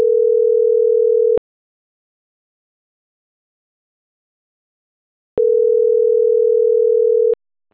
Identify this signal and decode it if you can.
Ringback tone (North America), 440 + 480 Hz.